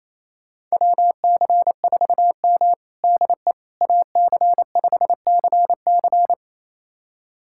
Morse code, heard as a keyed tone, about 28 wpm, 700 Hz. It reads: WC4M DE AC5CC
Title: USA Callsign DE Callsign 28wpm